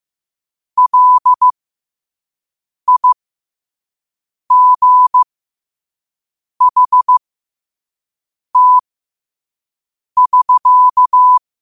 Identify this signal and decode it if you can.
Morse code, heard as a keyed tone, about 15 wpm, spaced as for 5 wpm, 1000 Hz. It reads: LIGHT<SK>
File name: qotdfiles / 5wpm